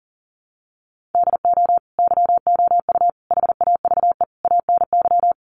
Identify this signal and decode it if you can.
Morse code, heard as a keyed tone, about 40 wpm, 700 Hz. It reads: DO YOU HAVE ANY
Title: Sets of 4 Words 40wpm